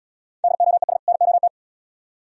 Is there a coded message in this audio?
Yes: Morse code from 0.4 to 1.5 s, heard as a keyed tone, about 70 wpm, 700 Hz. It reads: DOES NOT